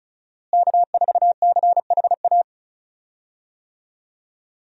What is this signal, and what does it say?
Morse code, heard as a keyed tone, about 34 wpm, 700 Hz. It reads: K4CHA
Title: USA Call Signs - 5 Characters - Repeated 2x - 35wpm